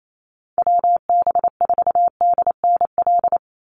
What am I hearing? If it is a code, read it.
Morse code, heard as a keyed tone, about 28 wpm, 700 Hz. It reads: WB4DNL
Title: Call Signs - 28wpm